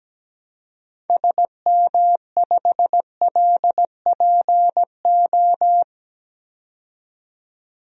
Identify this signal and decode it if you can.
Morse code, heard as a keyed tone, about 17 wpm, 700 Hz. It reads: SM5LPO
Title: DXCC-100 Call Signs - 6 Characters - 17wpm